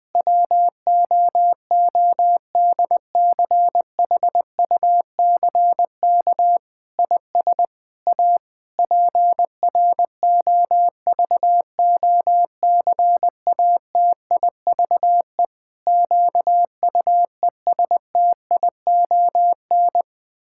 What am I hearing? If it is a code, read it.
Morse code, heard as a keyed tone, about 20 wpm, 700 Hz. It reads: WOODCHUCK IS A PROVOCATIVE QUESTION
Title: Pangram Challenge v2 - 20wpm